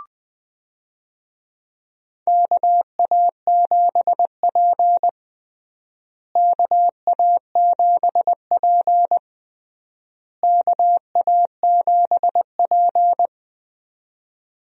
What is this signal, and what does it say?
Morse code, heard as a keyed tone, about 20 wpm, 700 Hz. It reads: KA7P KA7P KA7P
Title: USA Call Signs - 4 Characters - Repeated 3x - 20wpm